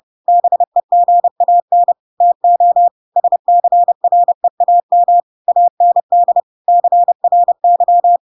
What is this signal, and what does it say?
Morse code, heard as a keyed tone, about 30 wpm, 700 Hz. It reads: BEGAN TO SCREAM AND CRY
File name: Pinocchio / Pinocchio ch7 30 30 ik1hge